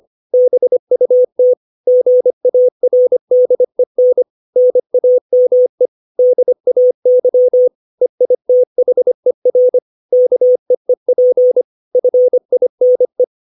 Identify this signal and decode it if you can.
Morse code, heard as a keyed tone, about 25 wpm, 500 Hz. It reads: BUT GARDEN NAME DAY EITHER KEEP FINE
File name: 632 words / 632 Words 25WPM